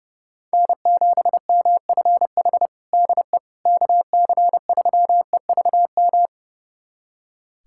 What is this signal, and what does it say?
Morse code, heard as a keyed tone, about 30 wpm, 700 Hz. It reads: N7MFH DE KC3EVM
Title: USA Callsign DE Callsign 30wpm